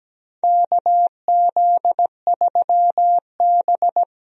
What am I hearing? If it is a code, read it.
Morse code, heard as a keyed tone, about 17 wpm, 700 Hz. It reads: KZ3B
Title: USA Call Signs - 4 Characters - 17wpm